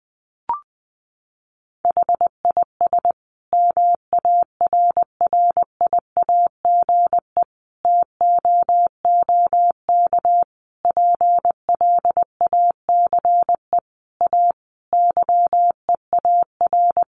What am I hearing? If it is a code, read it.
Morse code, heard as a keyed tone, about 20 wpm, 700 Hz. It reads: HIS MARRIAGE TOOK PLACE A YEAR
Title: Sentences from Top 1500 Words 20wpm